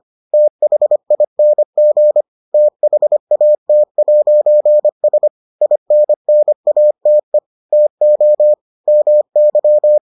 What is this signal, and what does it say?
Morse code, heard as a keyed tone, about 25 wpm, 600 Hz. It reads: THING THAT'S INNATE TO MY